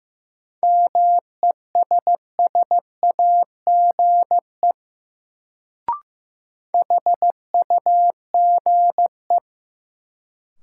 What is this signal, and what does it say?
Morse code, heard as a keyed tone, about 15 wpm, 700 Hz. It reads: MESSAGE HUGE
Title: Sentences from Top 1100 Words - New Word Drill - 15wpm